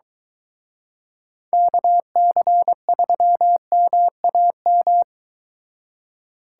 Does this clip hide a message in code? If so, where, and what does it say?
Yes: Morse code from 1.5 to 5.0 s, heard as a keyed tone, about 23 wpm, 700 Hz. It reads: KC3MAM